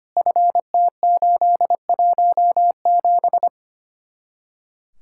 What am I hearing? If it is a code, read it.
Morse code, heard as a keyed tone, about 30 wpm, 700 Hz. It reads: FT817
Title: Speed-Racing Common QSO Phrases - 25wpm